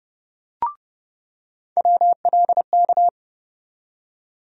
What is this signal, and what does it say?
Morse code, heard as a keyed tone, about 30 wpm, 700 Hz. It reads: WLK
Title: USA Call Sign Postfixes - 3 Letters 30wpm